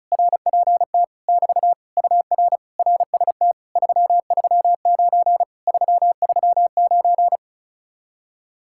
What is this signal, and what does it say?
Morse code, heard as a keyed tone, about 40 wpm, 700 Hz. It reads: RPT <BT> UR RST 339 339